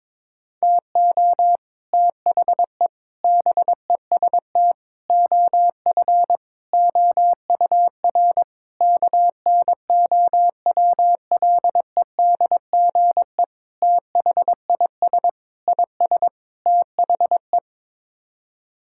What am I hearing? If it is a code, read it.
Morse code, heard as a keyed tone, about 22 wpm, 700 Hz. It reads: TO THE BEST OF OUR KNOWLEDGE THIS IS THE
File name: Sets of 9 Words - 22wpm